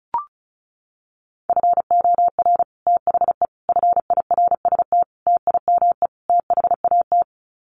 Morse code, heard as a keyed tone, about 34 wpm, 700 Hz. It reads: FOR THE FIRST TIME THAT
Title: Sets of 5 Words Encore 35wpm